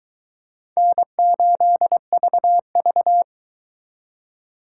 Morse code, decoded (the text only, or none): N8VV